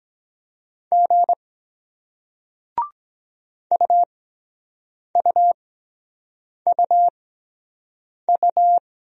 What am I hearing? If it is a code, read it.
Morse code, heard as a keyed tone, about 23 wpm, spaced as for 7 wpm, 700 Hz. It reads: G UUUU